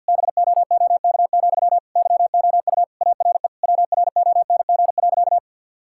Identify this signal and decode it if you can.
Morse code, heard as a keyed tone, about 50 wpm, 700 Hz. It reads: BOOK, YOU ARE WRONG.
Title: Sentences from Top 1000 Words - Review Part 5 - 50wpm